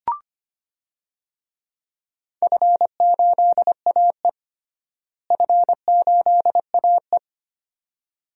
Morse code, heard as a keyed tone, about 25 wpm, 700 Hz. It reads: F8AE F8AE
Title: DXCC-100 Call Signs - 4 Characters - Repeated 2x - 25wpm